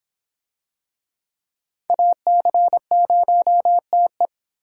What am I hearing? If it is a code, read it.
Morse code, heard as a keyed tone, about 26 wpm, 700 Hz. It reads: AC0TE